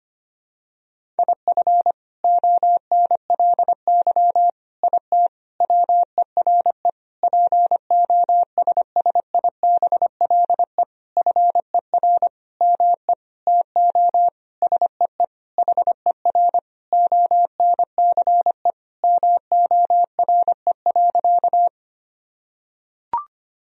Morse code, heard as a keyed tone, about 25 wpm, 700 Hz. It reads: IF ONLY IT WERE POSSIBLE FER ME TO SEE HER ONCE MORE.
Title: Sentences from Top 500 Words and Abbreviations 25wpm